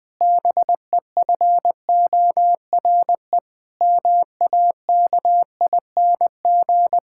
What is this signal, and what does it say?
Morse code, heard as a keyed tone, about 21 wpm, 700 Hz. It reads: BEFORE MAKING
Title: Difficult Sentences with Hints - 20wpm